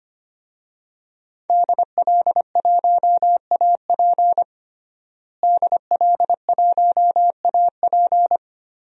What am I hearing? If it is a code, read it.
Morse code, heard as a keyed tone, about 25 wpm, 700 Hz. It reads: DL1AP DL1AP
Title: DXCC-100 Call Signs - 5 Characters - Repeated 2x - 25wpm